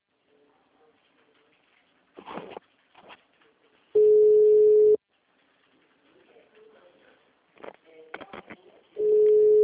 Ringback tone (Europe, 425 Hz).